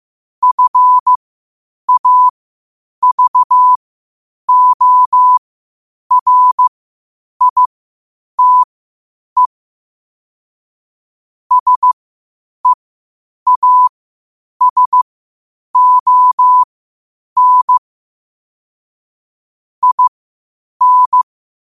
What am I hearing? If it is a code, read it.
Morse code, heard as a keyed tone, about 15 wpm, spaced as for 8 wpm, 1000 Hz. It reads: FAVORITE SEASON IN